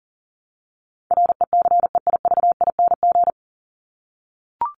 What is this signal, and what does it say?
Morse code, heard as a keyed tone, about 40 wpm, 700 Hz. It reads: RECEIVING
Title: ING Suffix - Top 100 Words with Base Words 40wpm